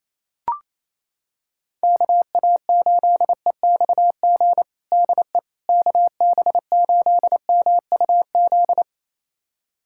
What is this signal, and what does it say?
Morse code, heard as a keyed tone, about 28 wpm, 700 Hz. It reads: KA8EXG DE KB8MUZ
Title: USA Callsign DE Callsign 28wpm